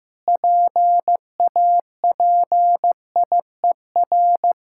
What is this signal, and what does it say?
Morse code, heard as a keyed tone, about 15 wpm, 700 Hz. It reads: PAPIER
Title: Pangram Challenge v2 - 15wpm